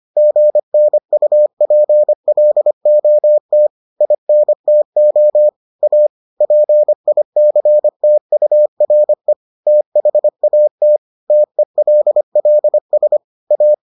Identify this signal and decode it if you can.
Morse code, heard as a keyed tone, about 25 wpm, 600 Hz. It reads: GNUPLOT INTO A PICTURE THAT TELLS A